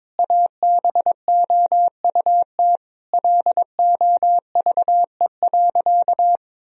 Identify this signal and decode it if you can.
Morse code, heard as a keyed tone, about 21 wpm, 700 Hz. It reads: ABOUT LOVE.
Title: Sentences from Top 1000 Words - Review Part 2 - 22wpm